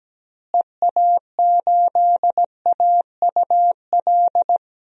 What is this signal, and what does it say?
Morse code, heard as a keyed tone, about 17 wpm, 700 Hz. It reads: EA8AUL
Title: DXCC-100 Call Signs - 6 Characters - Repeated 3x - 17wpm